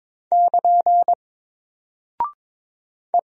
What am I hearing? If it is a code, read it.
Morse code, heard as a keyed tone, about 22 wpm, 700 Hz. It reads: <KN> E
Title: Single Character (Letter:Number:Punctuation:Prosign) 22wpm